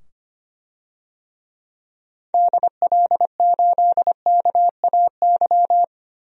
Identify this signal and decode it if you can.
Morse code, heard as a keyed tone, about 25 wpm, 700 Hz. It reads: DL8KAY
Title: DXCC-100 Call Signs - 6 Characters - 25wpm